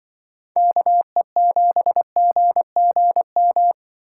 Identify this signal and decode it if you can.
Morse code, heard as a keyed tone, about 24 wpm, 700 Hz. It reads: KE7GGM
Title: Speed Racing - USA Call Signs 20wpm